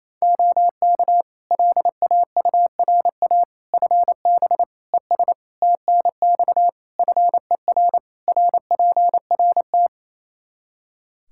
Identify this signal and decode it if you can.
Morse code, heard as a keyed tone, about 28 wpm, 700 Hz. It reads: OK LAURA FB ES TNX FER RPRT
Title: K1USN SST QSO Practice 28wpm